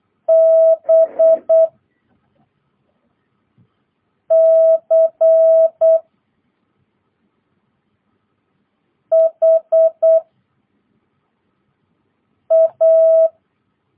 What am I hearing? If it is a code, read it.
Morse code, heard as a keyed tone, about 8 wpm, 650 Hz. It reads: B C H A
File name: jh8cbh / 2022cw